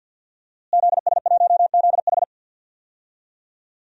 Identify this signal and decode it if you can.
Morse code, heard as a keyed tone, about 46 wpm, 700 Hz. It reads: ZS1ZH
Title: DXCC-100 Call Signs - 5 Characters - Repeated 2x - 50wpm